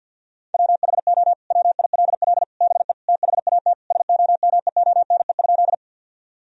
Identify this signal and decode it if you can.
Morse code, heard as a keyed tone, about 50 wpm, 700 Hz. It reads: WHO WILL BE THAT SOMEONE?
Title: Sentences from Top 1000 Words - Review Part 4 - 50wpm